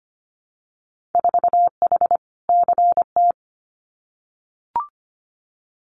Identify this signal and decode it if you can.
Morse code, heard as a keyed tone, about 25 wpm, 700 Hz. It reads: 4H CT
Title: Winter Field Day Exchanges Encore 25wpm